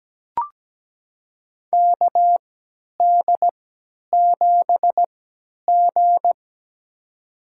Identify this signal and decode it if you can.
Morse code, heard as a keyed tone, about 17 wpm, spaced as for 10 wpm, 700 Hz. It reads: KD7G
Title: USA Call Signs - 4 Characters - 3x Character Spacing - 17wpm